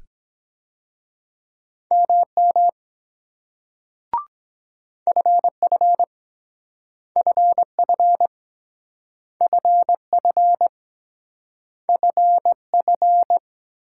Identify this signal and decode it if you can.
Morse code, heard as a keyed tone, about 23 wpm, 700 Hz. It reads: MM FF FF FF FF